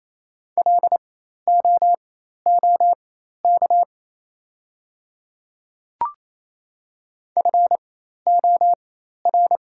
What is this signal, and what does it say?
Morse code, heard as a keyed tone, about 27 wpm, spaced as for 13 wpm, 700 Hz. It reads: LOOK FOR
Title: 4x Character Spacing - Top 100 Words - 28wpm